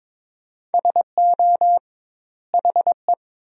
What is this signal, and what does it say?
Morse code, heard as a keyed tone, about 23 wpm, 700 Hz. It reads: SO HE
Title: 2x Word-Spacing - Sentences from Top 300 Words - 22wpm